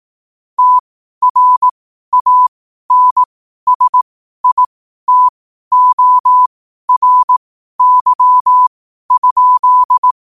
Morse code, heard as a keyed tone, about 18 wpm, spaced as for 12 wpm, 1000 Hz. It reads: TRANSITORY?